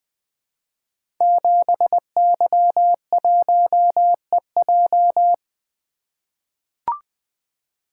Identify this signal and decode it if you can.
Morse code, heard as a keyed tone, about 20 wpm, 700 Hz. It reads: 7Y1EJ
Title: Code Groups - 5 Letter:Numbers - 20wpm